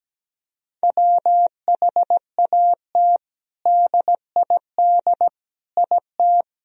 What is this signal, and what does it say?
Morse code, heard as a keyed tone, about 17 wpm, 700 Hz. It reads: WHAT DID IT
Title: Sentences from Top 1000 Words - Review Part 3 - 17wpm